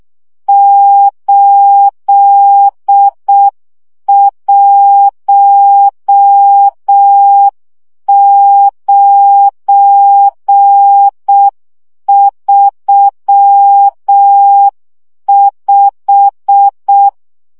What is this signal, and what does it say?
Morse code, heard as a keyed tone, about 6 wpm, 800 Hz. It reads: 81935
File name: morze vizsga 2011-3 5